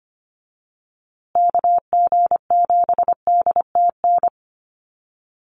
Morse code, heard as a keyed tone, about 25 wpm, 700 Hz. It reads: KG7DTN
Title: USA Call Signs Repeated 2x 25wpm